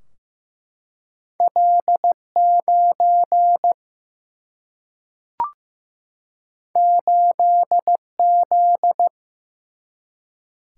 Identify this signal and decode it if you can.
Morse code, heard as a keyed tone, about 15 wpm, 700 Hz. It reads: L9 8Z
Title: Single-Letter Single-Number 15wpm